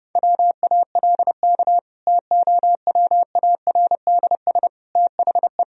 Morse code, heard as a keyed tone, about 30 wpm, 700 Hz. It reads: WALK TOWARDS THE
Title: Sentences from Top 1000 Words - Review Part 2 - 30wpm